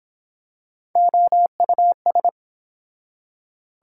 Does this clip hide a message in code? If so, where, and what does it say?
Yes: Morse code from 1.0 to 2.3 s, heard as a keyed tone, about 24 wpm, 700 Hz. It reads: OUS